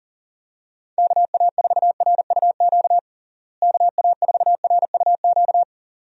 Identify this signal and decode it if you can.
Morse code, heard as a keyed tone, about 37 wpm, 700 Hz. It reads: KA4RUQ KA4RUQ
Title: USA Call Signs - 6 Characters - Repeated 2x - 40wpm